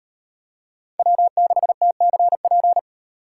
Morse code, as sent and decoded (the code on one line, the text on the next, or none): .-- -.... - -.-. .--.
W6TCP